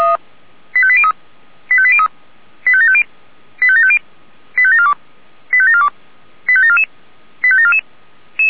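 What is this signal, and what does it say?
Five-tone selective call (ZVEI), heard as quick runs of five tones, tones of about 70 ms. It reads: 85803 85803 85750 85750 85742 85742 85744 85744
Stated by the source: ZVEI five-tone selective call received by SDR.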